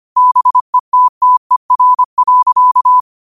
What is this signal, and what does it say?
Morse code, heard as a keyed tone, about 25 wpm, 1000 Hz. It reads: BETTER.